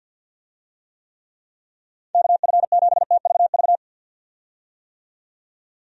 Morse code, heard as a keyed tone, about 50 wpm, 700 Hz. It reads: KF7TVV
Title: USA Call Signs - 6 Characters - 50wpm